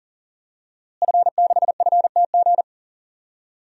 Morse code, heard as a keyed tone, about 40 wpm, 700 Hz. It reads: F6FTG
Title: DXCC-100 Call Signs - 5 Characters - Repeated 2x - 40wpm